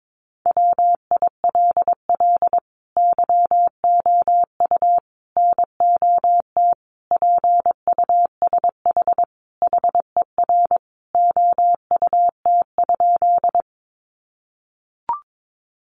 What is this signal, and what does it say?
Morse code, heard as a keyed tone, about 22 wpm, 700 Hz. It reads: WILL YOU NOT PUSH HER OUT?
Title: Sentences from Top 1300 Words 22wpm